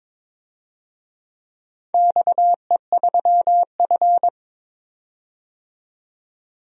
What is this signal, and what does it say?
Morse code, heard as a keyed tone, about 21 wpm, 700 Hz. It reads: XE3F
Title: DXCC-100 Call Signs - 4 Characters - 22wpm